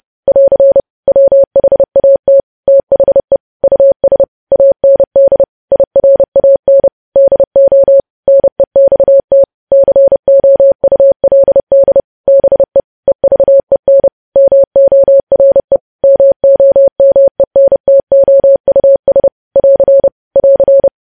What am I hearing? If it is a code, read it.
Morse code, heard as a keyed tone, about 30 wpm, 550 Hz. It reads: <AR> WHAT THE US AND IRAN DO NEXT COULD BE EVEN MORE MOMENTOUS <AR> <AR>
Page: news headlines in morse code at 30 wpm